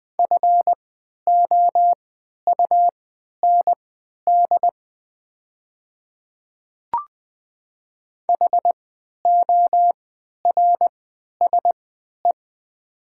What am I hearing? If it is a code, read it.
Morse code, heard as a keyed tone, about 21 wpm, spaced as for 11 wpm, 700 Hz. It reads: FOUND HORSE
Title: 3x Character Spacing - Top 500 Words - 20wpm